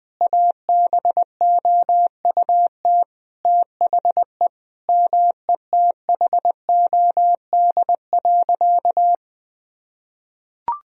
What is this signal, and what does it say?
Morse code, heard as a keyed tone, about 20 wpm, 700 Hz. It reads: ABOUT THE METHOD.
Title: Sentences from Top 1000 Words - Review Part 3 - 20wpm